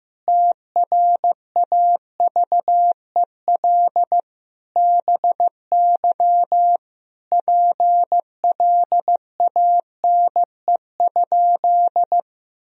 Morse code, heard as a keyed tone, about 15 wpm, 700 Hz. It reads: TRAVEL BY PLANE?
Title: Sentences from Top 1000 Words - Review Part 3 - 15wpm